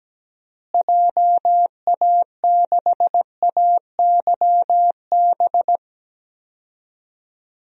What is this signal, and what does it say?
Morse code, heard as a keyed tone, about 17 wpm, 700 Hz. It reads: JA6AYB
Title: DXCC-100 Call Signs - 6 Characters - Repeated 2x - 17wpm